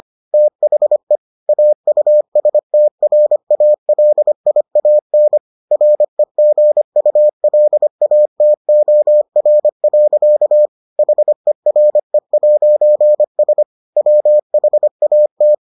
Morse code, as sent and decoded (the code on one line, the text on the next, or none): - .... / .- ..- ... - .-. .- .-.. .. .- -. / .-. --. ..- .-.. .- - --- .-. .-.-.- / .... .-. .----. ... / .-- .... .- -
THE AUSTRALIAN REGULATOR. HERE'S WHAT